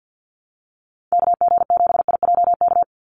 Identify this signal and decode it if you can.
Morse code, heard as a keyed tone, about 49 wpm, 700 Hz. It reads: KG7IJK
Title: USA Call Signs 50wpm